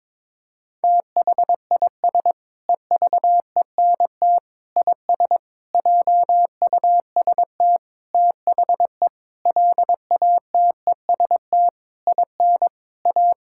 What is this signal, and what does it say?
Morse code, heard as a keyed tone, about 22 wpm, 700 Hz. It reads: THIS EVENT IS JUST THE LATEST IN A
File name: practice / 22wpm / Sentences with 8-Word Phrases - 22wpm